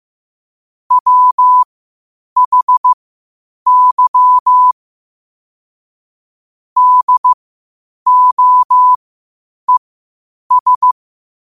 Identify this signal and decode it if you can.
Morse code, heard as a keyed tone, about 15 wpm, spaced as for 8 wpm, 1000 Hz. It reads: WHY DOES
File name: qotdfiles / 8wpm